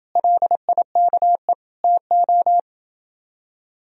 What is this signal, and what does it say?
Morse code, heard as a keyed tone, about 29 wpm, 700 Hz. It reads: LIKE TO